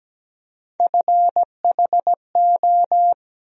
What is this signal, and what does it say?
Morse code, heard as a keyed tone, about 17 wpm, 700 Hz. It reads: FHO
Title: USA Call Sign Postfixes - 3 Letters 17wpm